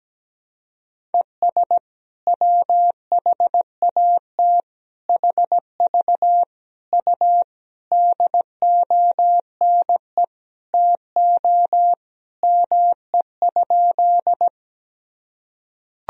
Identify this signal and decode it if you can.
Morse code, heard as a keyed tone, about 17 wpm, 700 Hz. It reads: ES WHAT HV U DONE TO ME?
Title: Sentences from Top 500 Words and Abbreviations 17wpm